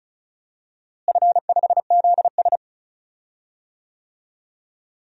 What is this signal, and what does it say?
Morse code, heard as a keyed tone, about 36 wpm, 700 Hz. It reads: F5ZS
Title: DXCC-100 Call Signs - 4 Characters - Repeated 2x - 35wpm